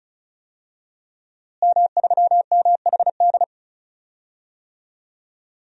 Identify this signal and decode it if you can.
Morse code, heard as a keyed tone, about 36 wpm, 700 Hz. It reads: M3MHD
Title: DXCC-100 Call Signs - 5 Characters - Repeated 3x - 35wpm